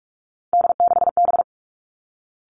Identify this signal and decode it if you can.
Morse code, heard as a keyed tone, about 46 wpm, 700 Hz. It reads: D6B
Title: Three Challenging Letter-Number Combinations 45wpm